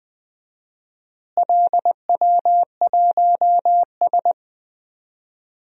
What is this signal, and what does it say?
Morse code, heard as a keyed tone, about 20 wpm, 700 Hz. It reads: LW1S
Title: DXCC-100 Call Signs - 4 Characters - Repeated 2x - 20wpm